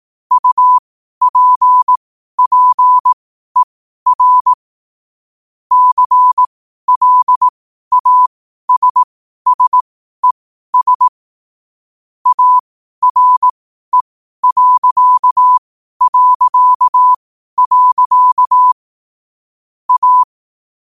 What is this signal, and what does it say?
Morse code, heard as a keyed tone, about 18 wpm, spaced as for 12 wpm, 1000 Hz. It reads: UPPER CLASSES ARE... A
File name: qotdfiles / 12wpm